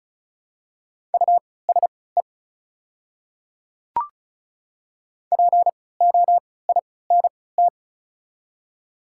Morse code, heard as a keyed tone, about 36 wpm, spaced as for 20 wpm, 700 Hz. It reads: USE POINT
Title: 3x Character Spacing - Top 200 Words - 35wpm